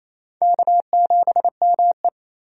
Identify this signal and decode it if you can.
Morse code, heard as a keyed tone, about 27 wpm, 700 Hz. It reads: K7ME
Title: USA Call Signs - 4 Characters - 28wpm